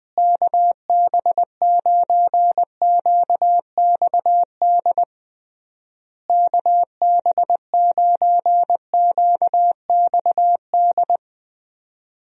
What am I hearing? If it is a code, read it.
Morse code, heard as a keyed tone, about 20 wpm, 700 Hz. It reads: KB9QXD KB9QXD
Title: USA Call Signs - 6 Characters - Repeated 3x - 20wpm